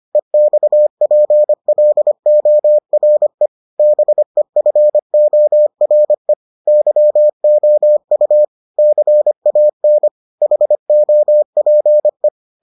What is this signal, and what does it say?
Morse code, heard as a keyed tone, about 25 wpm, 600 Hz. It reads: EXPLORE BEFORE YOU CAN HOPE